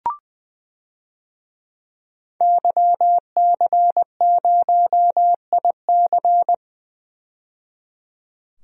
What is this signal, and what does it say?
Morse code, heard as a keyed tone, about 20 wpm, 700 Hz. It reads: YC0IC
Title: DXCC-100 Call Signs - 5 Characters - 20wpm